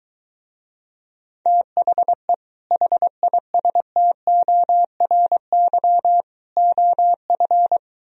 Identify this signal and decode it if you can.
Morse code, heard as a keyed tone, about 23 wpm, 700 Hz. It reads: THE HISTORY OF